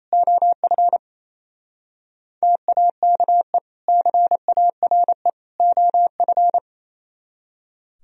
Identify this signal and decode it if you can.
Morse code, heard as a keyed tone, about 35 wpm, 700 Hz. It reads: OF TAKE CARE OF
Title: Speed-Racing - Sets of 3 Words 28wpm